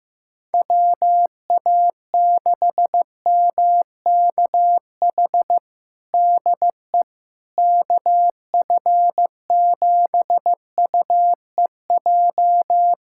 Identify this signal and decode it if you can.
Morse code, heard as a keyed tone, about 15 wpm, 700 Hz. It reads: WA6MKH DE KF7UEJ
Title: K1USN SST QSO Practice 15wpm